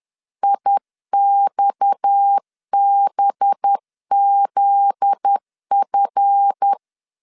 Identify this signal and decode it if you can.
Morse code, heard as a keyed tone, about 10 wpm, 800 Hz. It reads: IXBZF
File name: LEZIONI / 9 2 10wpm 2736